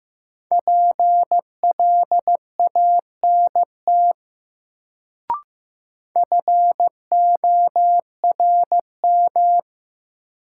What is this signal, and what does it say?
Morse code, heard as a keyed tone, about 15 wpm, 700 Hz. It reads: PLANT FORM
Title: Sentences from Top 1000 Words - Review Part 1 - New Word Drill - 15wpm